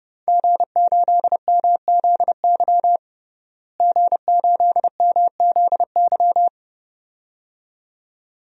Morse code, heard as a keyed tone, about 30 wpm, 700 Hz. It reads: G8MZY G8MZY
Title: DXCC-100 Call Signs - 5 Characters - Repeated 2x - 30wpm